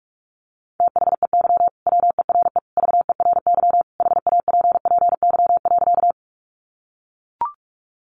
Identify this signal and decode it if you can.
Morse code, heard as a keyed tone, about 44 wpm, 700 Hz. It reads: THEY WERE VERY HAPPY.
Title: Sentences from Top 1000 Words 45wpm